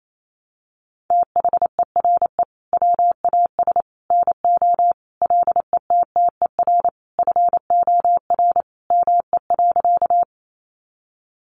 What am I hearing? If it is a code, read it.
Morse code, heard as a keyed tone, about 28 wpm, 700 Hz. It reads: THERE WAS NO LETTER FOR ME.
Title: Sentences from Top 300 Words 28wpm